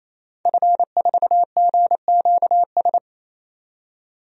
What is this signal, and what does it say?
Morse code, heard as a keyed tone, about 28 wpm, 700 Hz. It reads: F4GQS
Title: DXCC-100 Call Signs - 5 Characters - 28wpm